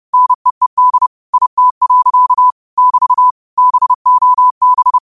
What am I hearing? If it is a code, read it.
Morse code, heard as a keyed tone, about 30 wpm, 1000 Hz. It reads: NEED IT. <BT> BOB
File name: 30wpm